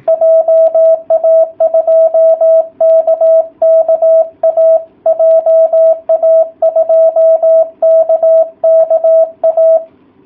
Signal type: Morse code